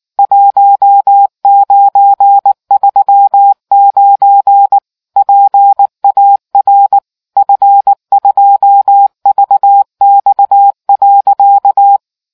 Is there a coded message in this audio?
Yes: Morse code from 0.2 to 12.0 s, heard as a keyed tone, about 19 wpm, 800 Hz. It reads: 1939 PAR F2VX.